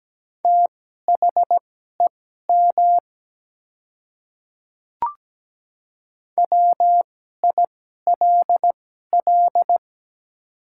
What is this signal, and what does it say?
Morse code, heard as a keyed tone, about 17 wpm, spaced as for 12 wpm, 700 Hz. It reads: THEM WILL
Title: 2x Character Spacing - Top 100 Words - 17wpm